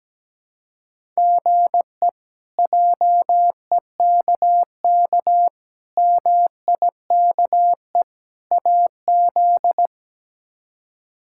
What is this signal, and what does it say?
Morse code, heard as a keyed tone, about 17 wpm, 700 Hz. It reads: GE JEKK MIKE AZ
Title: K1USN SST QSO Practice - 17wpm